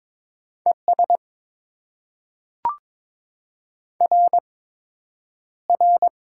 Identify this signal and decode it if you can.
Morse code, heard as a keyed tone, about 22 wpm, spaced as for 9 wpm, 700 Hz. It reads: ES RR
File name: Top 90 QSO Elements - 8x Character Spacing - 22wpm